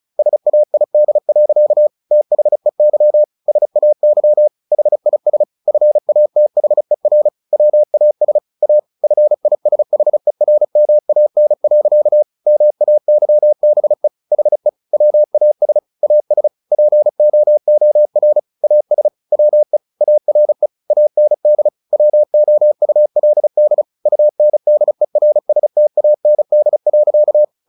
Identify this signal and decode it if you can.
Morse code, heard as a keyed tone, about 34 wpm, 600 Hz. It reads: SAID. THEY SAY HIS FATHER WAS A FISHERMAN. MAYBE HE WAS AS POOR AS WE ARE AND WOULD UNDERSTAND.
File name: the old man and the sea / the old man and the sea 1 35wpm